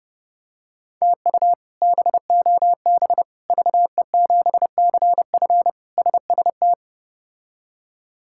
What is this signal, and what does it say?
Morse code, heard as a keyed tone, about 30 wpm, 700 Hz. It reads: TU BOB VE7CF SST
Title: K1USN SST QSO Practice - 30wpm